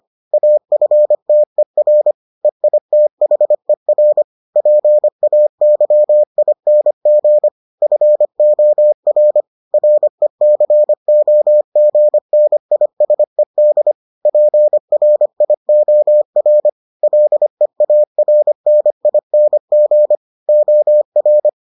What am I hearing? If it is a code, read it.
Morse code, heard as a keyed tone, about 25 wpm, 600 Hz. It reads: AFTER EITHER PAYING FOR RECOGNISED PRIOR LEARNING OR